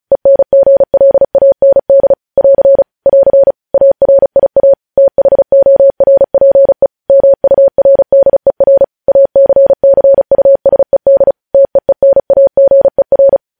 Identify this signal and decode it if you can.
Morse code, heard as a keyed tone, about 36 wpm, 550 Hz. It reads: ENGLAND <AR> <AR> ARIA THORPE MURDER ACCUSED TEENAGER
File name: podcast / News headlines-1x-35x35